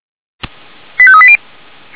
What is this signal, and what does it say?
Five-tone selective call (ZVEI), heard as quick runs of five tones, tones of about 70 ms. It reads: 85380
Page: SDR/ZVEI